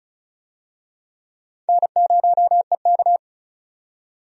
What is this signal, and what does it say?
Morse code, heard as a keyed tone, about 36 wpm, 700 Hz. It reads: N0EK
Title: USA Call Signs - 4 Characters - 35wpm